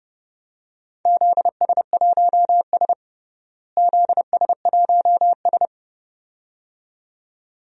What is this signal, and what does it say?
Morse code, heard as a keyed tone, about 32 wpm, 700 Hz. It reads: ZS1S ZS1S
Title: DXCC-100 Call Signs - 4 Characters - Repeated 2x - 30wpm